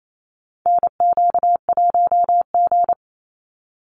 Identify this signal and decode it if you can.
Morse code, heard as a keyed tone, about 28 wpm, 700 Hz. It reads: NQ1G
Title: USA Call Signs 28wpm